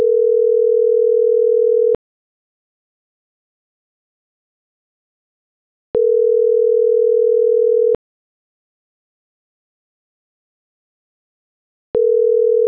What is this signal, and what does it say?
Ringback tone (North America), 440 + 480 Hz.